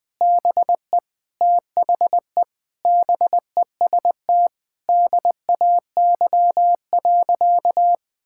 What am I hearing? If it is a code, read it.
Morse code, heard as a keyed tone, about 21 wpm, 700 Hz. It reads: BE THE BEST DAY.
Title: Sentences with 4-Word Phrases - 20wpm